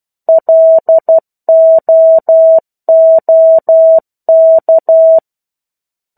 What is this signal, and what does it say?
Morse code, heard as a keyed tone, about 12 wpm, 650 Hz. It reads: LOOK
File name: 01-Chapter-01 12 WPM